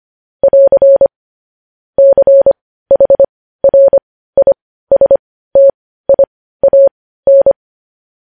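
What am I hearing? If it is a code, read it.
Morse code, heard as a keyed tone, about 25 wpm, spaced as for 15 wpm, 550 Hz. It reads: <AR> CHRISTIAN
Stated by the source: Morse news headlines practice at 15 wpm.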